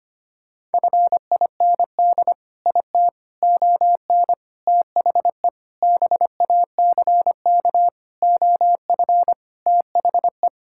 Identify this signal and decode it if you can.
Morse code, heard as a keyed tone, about 25 wpm, 700 Hz. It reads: FIND IT ON THE BACK OF THE
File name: practice / 25wpm / Sentences with 4-Word Phrases - 25wpm